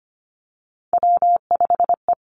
Morse code, heard as a keyed tone, about 25 wpm, 700 Hz. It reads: W5E
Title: Three Challenging Letter-Number Combinations 25wpm